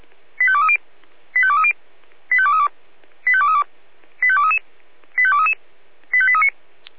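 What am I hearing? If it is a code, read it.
Five-tone selective call (ZVEI), heard as quick runs of five tones, tones of about 70 ms. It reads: 85230 85230 85232 85232 85233 85233 85839
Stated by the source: ZVEI five-tone call received on SDR.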